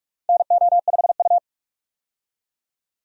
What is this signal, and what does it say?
Morse code, heard as a keyed tone, about 43 wpm, 700 Hz. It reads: NO5U
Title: USA Call Signs - 4 Characters - 45wpm